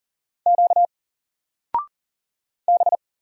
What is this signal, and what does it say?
Morse code, heard as a keyed tone, about 40 wpm, 700 Hz. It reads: Q B
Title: Single Letter-Number 40wpm